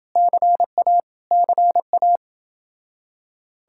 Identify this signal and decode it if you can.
Morse code, heard as a keyed tone, about 30 wpm, 700 Hz. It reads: CA CA